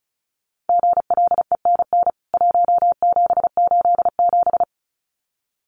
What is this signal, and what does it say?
Morse code, heard as a keyed tone, about 34 wpm, 700 Hz. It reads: GLENN 1787
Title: CWT Exchange 35wpm